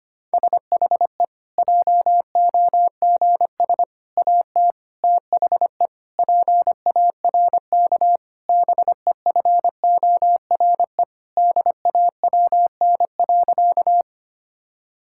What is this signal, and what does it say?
Morse code, heard as a keyed tone, about 25 wpm, 700 Hz. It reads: SHE JOGS AT THE PARK BEFORE DAWN.
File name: practice / 25wpm / Difficult Sentences with Hints - 25wpm